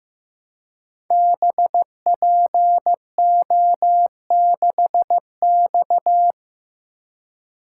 Morse code, heard as a keyed tone, about 15 wpm, 700 Hz. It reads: BPO6X
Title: Code Groups - 5 Letter:Numbers - 15wpm